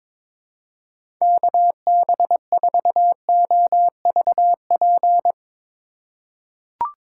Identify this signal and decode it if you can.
Morse code, heard as a keyed tone, about 22 wpm, 700 Hz. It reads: KB4OVP
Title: USA Call Signs - 6 Characters - 4x Character Spacing - 22wpm